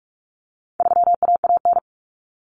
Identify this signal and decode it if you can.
Morse code, heard as a keyed tone, about 44 wpm, 700 Hz. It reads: 3AAN
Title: Common Cut Numbers - 4-Digit 45wpm